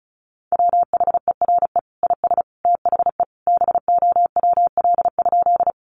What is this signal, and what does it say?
Morse code, heard as a keyed tone, about 34 wpm, 700 Hz. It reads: WHERE IS THE BOWL?
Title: Sentences from Top 1200 Words 35wpm